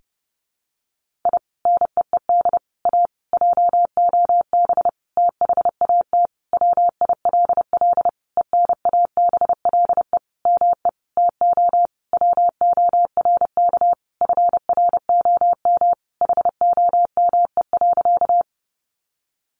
Morse code, heard as a keyed tone, about 28 wpm, 700 Hz. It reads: I NEED A JOB THAT WILL ENABLE ME TO WORK FROM HOME.
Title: Sentences from Top 1500 Words 30wpm